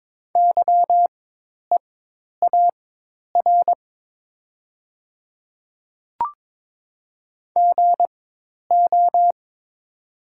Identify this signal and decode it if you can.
Morse code, heard as a keyed tone, about 22 wpm, spaced as for 10 wpm, 700 Hz. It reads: YEAR GO
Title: 4x Character Spacing - Top 200 Words - 22wpm